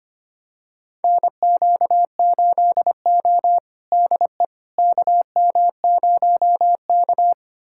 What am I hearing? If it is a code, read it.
Morse code, heard as a keyed tone, about 25 wpm, 700 Hz. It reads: NQ8O DE KM0K
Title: USA Callsign DE Callsign - 4 Characters - 25wpm